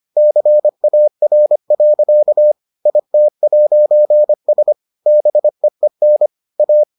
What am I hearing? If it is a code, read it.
Morse code, heard as a keyed tone, about 25 wpm, 600 Hz. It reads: CAR. IT'S BEEN A